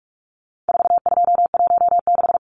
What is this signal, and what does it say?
Morse code, heard as a keyed tone, about 44 wpm, 700 Hz. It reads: 4216
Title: Numbers - 4-Digit 45wpm